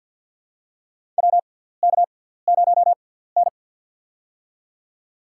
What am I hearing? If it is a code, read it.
Morse code, heard as a keyed tone, about 52 wpm, spaced as for 17 wpm, 700 Hz. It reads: WK0N
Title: USA Call Signs - 6x Character Spacing - 50wpm